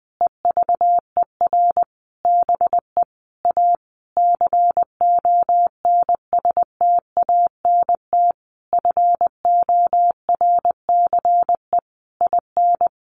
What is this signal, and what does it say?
Morse code, heard as a keyed tone, about 20 wpm, 700 Hz. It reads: EVER BE A CONSTANT FORCE IN
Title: Sentences from Top 1500 Words 20wpm